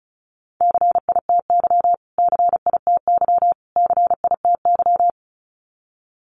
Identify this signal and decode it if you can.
Morse code, heard as a keyed tone, about 34 wpm, 700 Hz. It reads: CITY CITY CITY
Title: Top 500 Words Repeated 35wpm